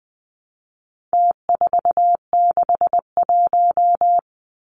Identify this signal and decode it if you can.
Morse code, heard as a keyed tone, about 20 wpm, 700 Hz. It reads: T461
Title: Numbers - Contest Style - 4-Digit 20wpm